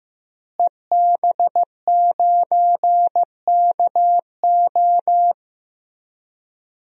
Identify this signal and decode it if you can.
Morse code, heard as a keyed tone, about 16 wpm, 700 Hz. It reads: EB9KO
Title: DXCC-100 Call Signs - 5 Characters - Repeated 3x - 15wpm